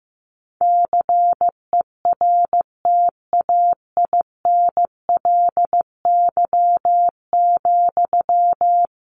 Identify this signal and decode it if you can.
Morse code, heard as a keyed tone, about 15 wpm, 700 Hz. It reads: CERTAINLY,
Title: Sentences from Top 1500 Words 15wpm